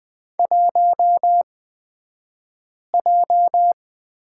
Morse code, heard as a keyed tone, about 20 wpm, spaced as for 5 wpm, 700 Hz. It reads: 1J
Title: 228 Lesson 37d - Letter x - Two Characters 20wpm